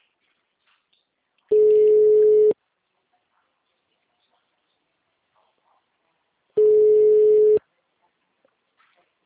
Ringback tone (Europe, 425 Hz).